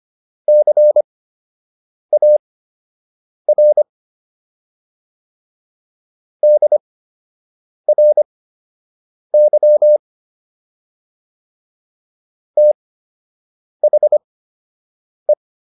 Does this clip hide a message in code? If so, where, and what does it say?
Yes: Morse code from 0.5 to 15.3 s, heard as a keyed tone, about 25 wpm, spaced as for 7 wpm, 600 Hz. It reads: CAR DRY THE